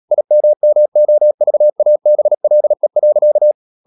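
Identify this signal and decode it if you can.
Morse code, heard as a keyed tone, about 37 wpm, 600 Hz. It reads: IMMOVABLE.
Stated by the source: Morse code practice at 35–39 wpm sending English text of Ken Follett's novel Never, from CW med Gnister.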